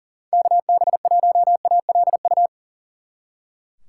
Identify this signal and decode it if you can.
Morse code, heard as a keyed tone, about 49 wpm, 700 Hz. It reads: KB1ALU
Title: Speed Racing - USA 2x3 Call Signs 40wpm